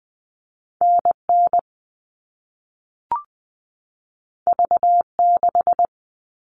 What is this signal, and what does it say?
Morse code, heard as a keyed tone, about 20 wpm, 700 Hz. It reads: NN V6
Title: All Cut Numbers - 2-Digit 20wpm